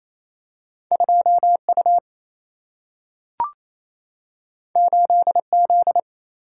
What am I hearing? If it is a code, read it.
Morse code, heard as a keyed tone, about 28 wpm, 700 Hz. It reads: 2U 8Z
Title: Single-Letter Single-Number 28wpm